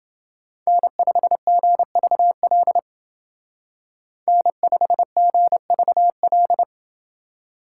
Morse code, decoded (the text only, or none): N5GVL N5GVL